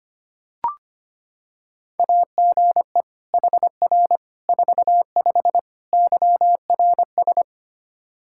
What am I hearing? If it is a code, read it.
Morse code, heard as a keyed tone, about 25 wpm, 700 Hz. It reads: AGE HR 45 YRS
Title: Practice QSO - Exchange 3 - More Info - 25wpm